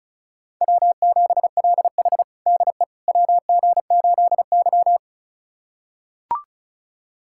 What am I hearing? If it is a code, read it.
Morse code, heard as a keyed tone, about 34 wpm, 700 Hz. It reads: W7LH DE WG8Y
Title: USA Callsign DE Callsign - 4 Characters - 35wpm